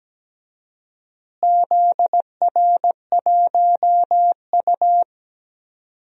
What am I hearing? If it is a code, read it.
Morse code, heard as a keyed tone, about 17 wpm, 700 Hz. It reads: ZR1U
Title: DXCC-100 Call Signs - 4 Characters - Repeated 2x - 17wpm